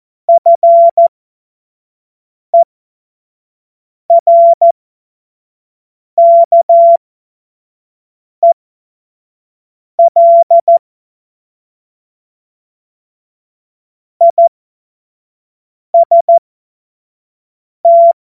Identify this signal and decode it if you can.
Morse code, heard as a keyed tone, about 14 wpm, spaced as for 5 wpm, 680 Hz. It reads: FERKEL IST
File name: Morsekurs / Witze / Witz 21. September 2021, 25 F70